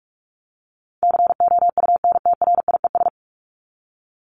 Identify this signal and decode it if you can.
Morse code, heard as a keyed tone, about 44 wpm, 700 Hz. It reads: COUNTRIES
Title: ES:S Suffix - Top 100 Words with Base Words 45wpm